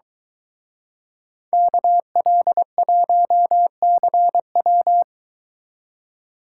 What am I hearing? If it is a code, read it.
Morse code, heard as a keyed tone, about 23 wpm, 700 Hz. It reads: KL1CW